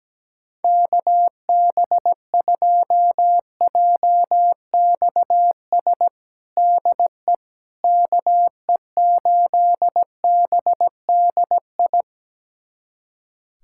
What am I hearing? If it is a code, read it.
Morse code, heard as a keyed tone, about 17 wpm, 700 Hz. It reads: KB2JXS DE KE8BDI
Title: USA Callsign DE Callsign 17wpm